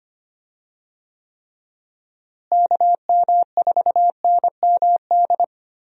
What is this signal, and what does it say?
Morse code, heard as a keyed tone, about 25 wpm, 700 Hz. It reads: KM4NMD
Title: USA Call Signs - 6 Characters - Repeated 2x - 25wpm